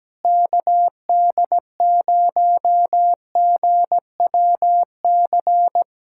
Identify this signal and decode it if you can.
Morse code, heard as a keyed tone, about 17 wpm, 700 Hz. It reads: KD0GWC
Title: USA Callsign DE Callsign - 6 Characters - 17wpm